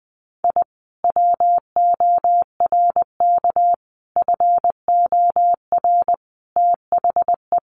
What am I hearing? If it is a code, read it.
Morse code, heard as a keyed tone, about 20 wpm, 700 Hz. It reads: I WORK FOR THE
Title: Sentences from Top 600 Words 20wpm